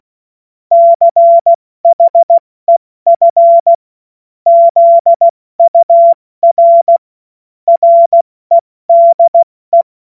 Morse code, heard as a keyed tone, about 16 wpm, spaced as for 14 wpm, 680 Hz. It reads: CHEF ZUR REDE
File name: Morsekurs / Witze / Witz 12 März 2024 70 F80